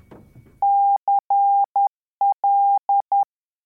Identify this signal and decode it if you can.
Morse code, heard as a keyed tone, about 10 wpm, 800 Hz. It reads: CL